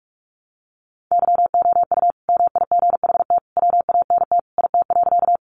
Morse code, heard as a keyed tone, about 44 wpm, 700 Hz. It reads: YOU MIGHT WANT IT.
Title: Sentences from Top 300 Words 45wpm